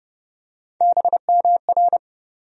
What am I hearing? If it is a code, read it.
Morse code, heard as a keyed tone, about 30 wpm, 700 Hz. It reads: BMR
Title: USA Call Sign Postfixes - 3 Letters 30wpm